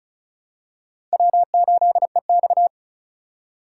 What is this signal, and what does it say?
Morse code, heard as a keyed tone, about 36 wpm, 700 Hz. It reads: W8EX
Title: USA Call Signs - 4 Characters - 3x Character Spacing - 35wpm